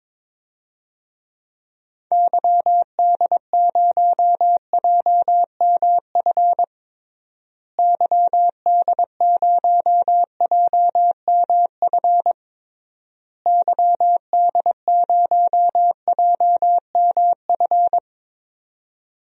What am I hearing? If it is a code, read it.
Morse code, heard as a keyed tone, about 22 wpm, 700 Hz. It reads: YD0JMF YD0JMF YD0JMF
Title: DXCC-100 Call Signs - 6 Characters - Repeated 3x - 22wpm